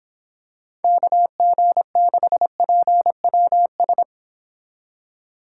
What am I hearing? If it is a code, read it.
Morse code, heard as a keyed tone, about 26 wpm, 700 Hz. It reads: KG6PWS